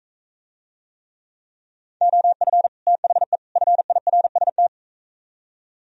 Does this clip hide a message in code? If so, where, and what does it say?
Yes: Morse code from 2.0 to 4.7 s, heard as a keyed tone, about 43 wpm, 700 Hz. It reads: OF THE FIRST